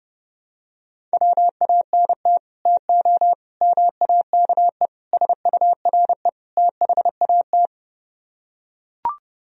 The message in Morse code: WANT TO MAKE SURE THAT

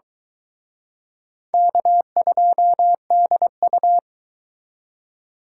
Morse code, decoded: K2DU

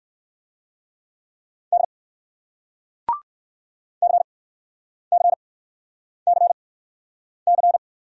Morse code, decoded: D CCCC

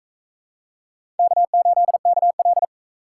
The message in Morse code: K8KL

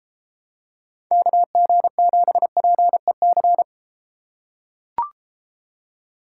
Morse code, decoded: KG7PEC

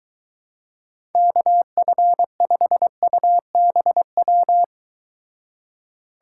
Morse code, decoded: KF5UBW

